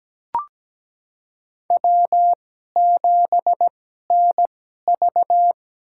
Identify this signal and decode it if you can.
Morse code, heard as a keyed tone, about 17 wpm, spaced as for 12 wpm, 700 Hz. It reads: W7NV
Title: USA Call Signs - 4 Characters - 2x Character Spacing - 17wpm